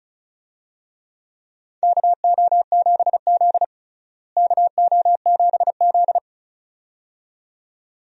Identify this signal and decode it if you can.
Morse code, heard as a keyed tone, about 34 wpm, 700 Hz. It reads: KO7Z KO7Z
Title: USA Call Signs - 4 Characters - Repeated 2x - 35wpm